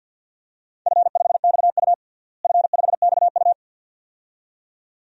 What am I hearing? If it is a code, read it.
Morse code, heard as a keyed tone, about 47 wpm, 700 Hz. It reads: F5XU F5XU
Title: DXCC-100 Call Signs - 4 Characters - Repeated 2x - 50wpm